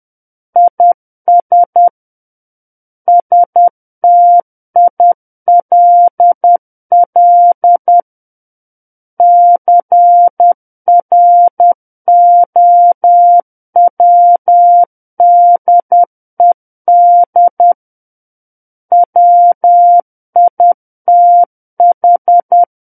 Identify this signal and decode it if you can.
Morse code, heard as a keyed tone, about 10 wpm, 700 Hz. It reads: IS STILL CROWDED WITH
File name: Book One / 01-Chapter 1 - 10 WPM